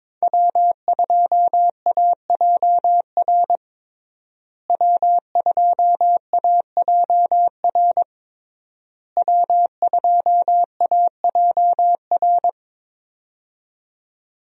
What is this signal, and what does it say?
Morse code, heard as a keyed tone, about 22 wpm, 700 Hz. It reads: W2AJR W2AJR W2AJR
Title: USA Call Signs - 5 Characters - Repeated 3x - 22wpm